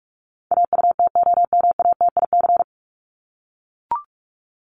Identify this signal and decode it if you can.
Morse code, heard as a keyed tone, about 44 wpm, 700 Hz. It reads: AUTOMATIC
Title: Top 1751-2000 Words 45wpm